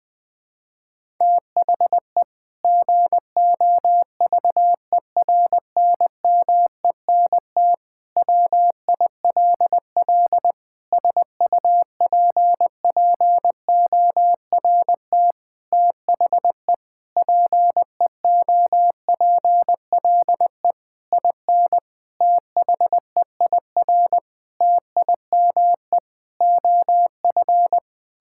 Morse code, heard as a keyed tone, about 21 wpm, 700 Hz. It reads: THE GOVERNMENT WILL SUPPORT THE PEOPLE IN THEIR TIME OF
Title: Sentences from Top 1000 Words - Review Part 5 - 20wpm